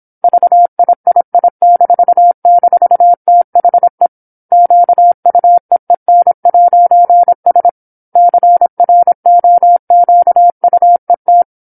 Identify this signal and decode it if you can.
Morse code, heard as a keyed tone, about 27 wpm, 700 Hz. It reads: VIII--THE QUEEN'S CROQUET
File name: 08-Chapter-08 26 WPM